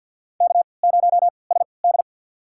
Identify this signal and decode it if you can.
Morse code, heard as a keyed tone, about 48 wpm, spaced as for 28 wpm, 700 Hz. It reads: K0SD
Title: USA Call Signs - 4 Characters - 3x Character Spacing - 50wpm